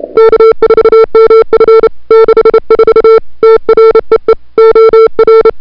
Morse code, heard as a keyed tone, about 31 wpm, 440 Hz. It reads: K4MF 64 TREE OR
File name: CWSprint / February 2007 CW Sprint / 01-2007 CW Sprint Hour1